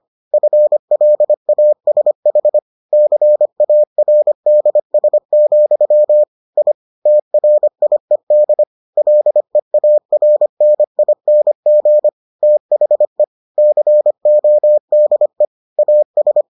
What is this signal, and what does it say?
Morse code, heard as a keyed tone, about 25 wpm, 600 Hz. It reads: FLASH CARDS, I TRIED LEARNING THE CODE AS